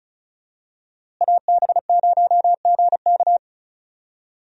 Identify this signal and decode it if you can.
Morse code, heard as a keyed tone, about 34 wpm, 700 Hz. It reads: AB0GK
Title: USA Call Signs - 5 Characters - Repeated 2x - 35wpm